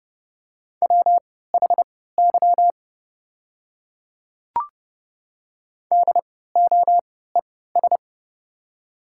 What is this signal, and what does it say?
Morse code, heard as a keyed tone, about 30 wpm, spaced as for 17 wpm, 700 Hz. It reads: WHY DOES
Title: 3x Character Spacing - Top 200 Words - 30wpm